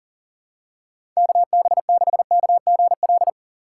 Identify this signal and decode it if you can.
Morse code, heard as a keyed tone, about 37 wpm, 700 Hz. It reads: KB6KGL
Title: USA Call Signs - 6 Characters - 3x Character Spacing - 40wpm